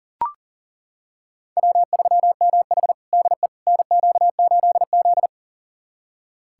Morse code, heard as a keyed tone, about 40 wpm, 700 Hz. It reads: W3MH DE NQ8Z
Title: USA Callsign DE Callsign - 4 Characters - 40wpm